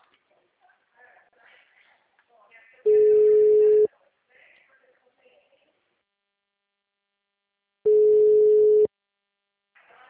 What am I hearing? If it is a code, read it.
Ringback tone (Europe, 425 Hz).